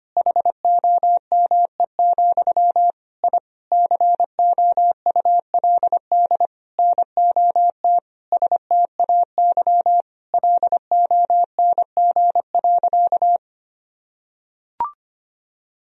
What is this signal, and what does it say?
Morse code, heard as a keyed tone, about 25 wpm, 700 Hz. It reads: HOME, I COULD NOT STAY LONG.
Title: Sentences from Top 1000 Words - Review Part 3 - 25wpm